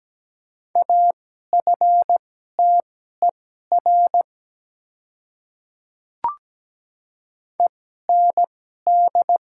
Morse code, heard as a keyed tone, about 17 wpm, spaced as for 12 wpm, 700 Hz. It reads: AFTER END
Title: 2x Character Spacing - Top 300 Words - 17wpm